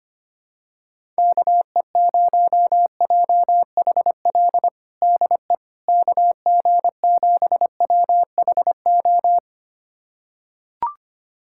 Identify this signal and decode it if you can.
Morse code, heard as a keyed tone, about 25 wpm, 700 Hz. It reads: KE0JHL DE KG7WHO
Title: USA Callsign DE Callsign 25wpm